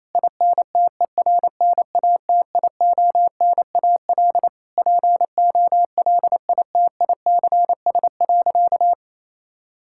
Morse code, heard as a keyed tone, about 28 wpm, 700 Hz. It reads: INTERNATIONAL POLITICS.
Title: Sentences from Top 1000 Words - Review Part 6 - 28wpm